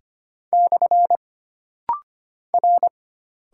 Morse code, heard as a keyed tone, about 25 wpm, 700 Hz. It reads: / R
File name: Single Character (Letter:Number:Punctuation:Prosign) - Rapid-Fire - 25wpm